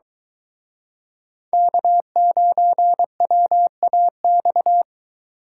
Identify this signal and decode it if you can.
Morse code, heard as a keyed tone, about 23 wpm, 700 Hz. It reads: K9WAX